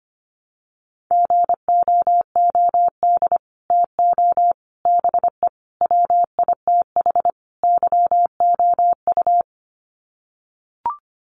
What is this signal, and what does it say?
Morse code, heard as a keyed tone, about 25 wpm, 700 Hz. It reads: GOOD TO BE WITH YOU
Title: Sets of 5 Words Encore2 25wpm